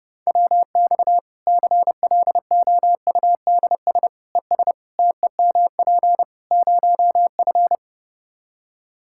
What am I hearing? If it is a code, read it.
Morse code, heard as a keyed tone, about 30 wpm, 700 Hz. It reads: WX CLOUDS ES TEMP 0F
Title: Practice QSO - Exchange 2 - Info - 30wpm